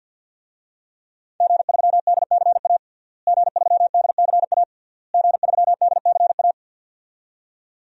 Morse code, heard as a keyed tone, about 50 wpm, 700 Hz. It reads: G3DCA G3DCA G3DCA
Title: DXCC-100 Call Signs - 5 Characters - Repeated 3x - 50wpm